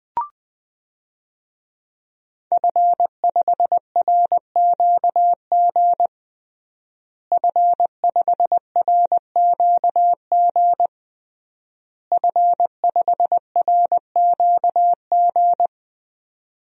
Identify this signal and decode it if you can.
Morse code, heard as a keyed tone, about 21 wpm, 700 Hz. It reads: F5RQG F5RQG F5RQG
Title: DXCC-100 Call Signs - 5 Characters - Repeated 3x - 20wpm